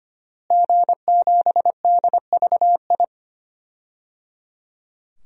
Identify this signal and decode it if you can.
Morse code, heard as a keyed tone, about 25 wpm, 700 Hz. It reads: G7DVI
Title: DXCC-100 Call Signs - 5 Characters - Repeated 3x - 25wpm